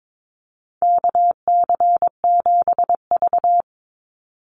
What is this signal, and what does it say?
Morse code, heard as a keyed tone, about 22 wpm, 700 Hz. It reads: KC7V
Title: Call Signs - 22wpm